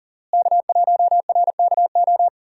Morse code, heard as a keyed tone, about 40 wpm, 700 Hz. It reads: K1RKO
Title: USA Call Signs - 5 Characters - Repeated 2x - 40wpm